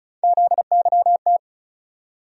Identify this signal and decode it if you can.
Morse code, heard as a keyed tone, about 34 wpm, 700 Hz. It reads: ZYT
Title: USA Call Sign Postfixes - 3 Letters 35wpm